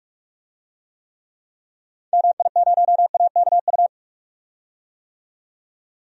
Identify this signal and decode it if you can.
Morse code, heard as a keyed tone, about 46 wpm, 700 Hz. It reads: MI0AKU
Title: DXCC-100 Call Signs - 6 Characters - 45wpm